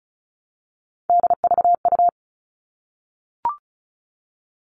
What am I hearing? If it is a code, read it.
Morse code, heard as a keyed tone, about 34 wpm, 700 Hz. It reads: DVU
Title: All Cut Numbers - 3-Digit 35wpm